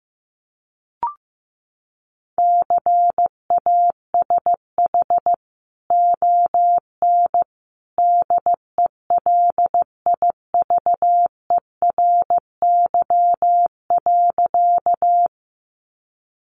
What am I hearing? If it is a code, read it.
Morse code, heard as a keyed tone, about 15 wpm, 700 Hz. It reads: CASH ON DELIVERY.
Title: Sentences from Top 1500 Words 15wpm